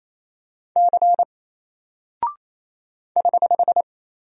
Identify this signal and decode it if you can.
Morse code, heard as a keyed tone, about 27 wpm, 700 Hz. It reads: C <HH>
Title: Single Character (Letter:Number:Punctuation:Prosign) 28wpm